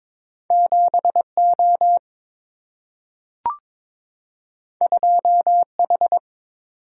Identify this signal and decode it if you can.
Morse code, heard as a keyed tone, about 22 wpm, 700 Hz. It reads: 7O 2H
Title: Single-Letter Single-Number 22wpm